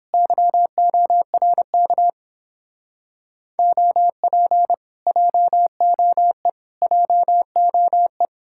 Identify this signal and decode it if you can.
Morse code, heard as a keyed tone, about 29 wpm, 700 Hz. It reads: YORK OP JOE JOE